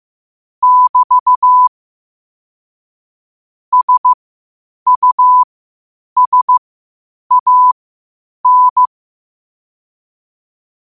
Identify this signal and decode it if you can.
Morse code, heard as a keyed tone, about 15 wpm, spaced as for 8 wpm, 1000 Hz. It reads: <BT> SUSAN
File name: qotdfiles / 8wpm